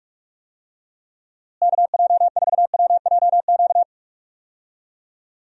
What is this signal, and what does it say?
Morse code, heard as a keyed tone, about 46 wpm, 700 Hz. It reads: KJ4WJQ
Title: USA Call Signs - 6 Characters - Repeated 2x - 45wpm